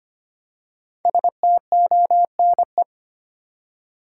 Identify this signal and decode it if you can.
Morse code, heard as a keyed tone, about 25 wpm, 700 Hz. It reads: STONE
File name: 5 Letter Words - First 3 Letters Given - 25wpm